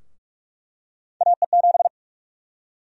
Morse code, heard as a keyed tone, about 46 wpm, 700 Hz. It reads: AE7
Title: USA Call Sign Prefixes - 45wpm